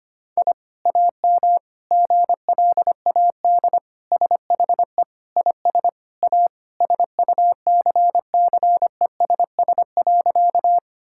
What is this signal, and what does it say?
Morse code, heard as a keyed tone, about 25 wpm, 700 Hz. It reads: I AM GLAD SHE IS A SUCCESS.
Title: Sentences from Top 1000 Words - Review Part 6 - 25wpm